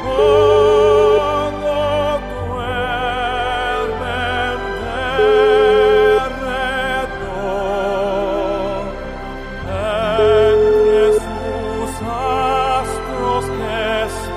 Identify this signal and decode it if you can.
Ringback tone (Europe, 425 Hz), with noise.